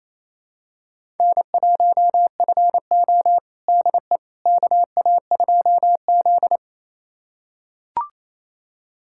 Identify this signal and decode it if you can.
Morse code, heard as a keyed tone, about 28 wpm, 700 Hz. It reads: N1FO DE KA2Z
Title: USA Callsign DE Callsign - 4 Characters - 28wpm